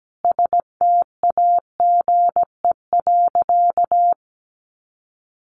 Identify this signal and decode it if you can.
Morse code, heard as a keyed tone, about 17 wpm, 700 Hz. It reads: STAGE.
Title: Sentences from Top 1200 Words 17wpm